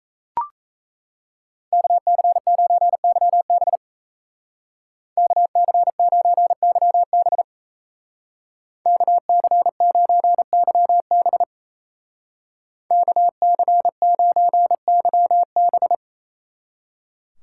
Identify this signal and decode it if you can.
Morse code, heard as a keyed tone, about 37 wpm, 700 Hz. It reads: KC9YB KC9YB KC9YB KC9YB